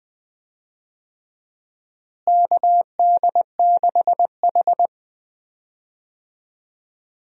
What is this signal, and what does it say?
Morse code, heard as a keyed tone, about 20 wpm, 700 Hz. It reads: KD6H
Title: USA Call Signs - 4 Characters - Repeated 3x - 20wpm